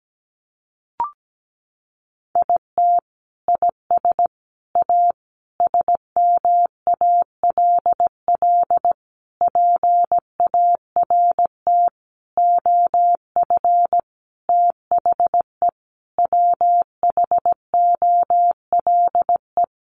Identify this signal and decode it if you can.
Morse code, heard as a keyed tone, about 17 wpm, 700 Hz. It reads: IT IS A SMALL PART OF THE WHOLE
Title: Sentences from Top 1500 Words 17wpm